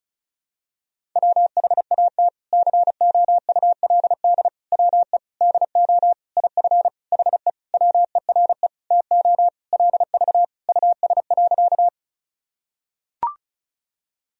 Morse code, heard as a keyed tone, about 36 wpm, 700 Hz. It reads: WHAT COULD WE DO IF HE WERE TO LV US.
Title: Sentences from Top 400 Words and Abbreviations 35wpm